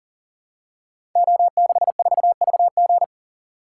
Morse code, heard as a keyed tone, about 42 wpm, 700 Hz. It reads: O64VG
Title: Code Groups - 5 Letter:Numbers - 40wpm